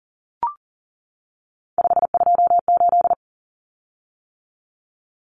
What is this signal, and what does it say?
Morse code, heard as a keyed tone, about 40 wpm, 700 Hz. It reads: 528
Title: Common Cut Numbers - 3-Digit 40wpm